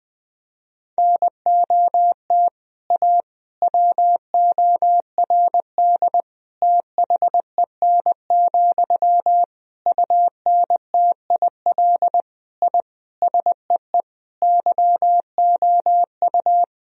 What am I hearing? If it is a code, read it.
Morse code, heard as a keyed tone, about 20 wpm, 700 Hz. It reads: NOT A WORD THEN, UNTIL I SEE YOU
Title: Sentences from Top 1000 Words - Review Part 1 - 20wpm